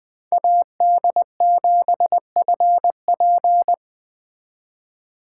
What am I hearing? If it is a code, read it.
Morse code, heard as a keyed tone, about 21 wpm, 700 Hz. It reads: AD7FP